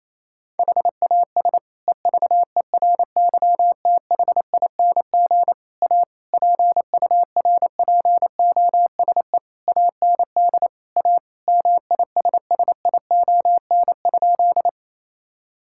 Morse code, heard as a keyed tone, about 28 wpm, 700 Hz. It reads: HAS EVERYTHING A PURPOSE AND A MISSION?
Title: Sentences from Top 1600 Words 28wpm